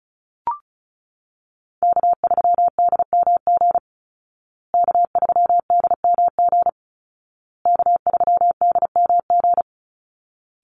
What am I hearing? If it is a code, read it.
Morse code, heard as a keyed tone, about 34 wpm, 700 Hz. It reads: K3DMG K3DMG K3DMG
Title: Call Signs Repeated 2x - 35wpm